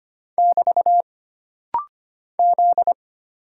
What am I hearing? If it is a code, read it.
Morse code, heard as a keyed tone, about 25 wpm, 700 Hz. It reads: <BT> Z
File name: Single Character (Letter:Number:Punctuation:Prosign) - Rapid-Fire - 25wpm